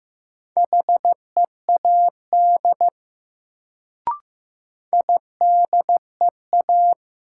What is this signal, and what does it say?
Morse code, heard as a keyed tone, about 15 wpm, 700 Hz. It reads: HEAD IDEA
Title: Sentences from Top 300 Words - New Word Drill - 15wpm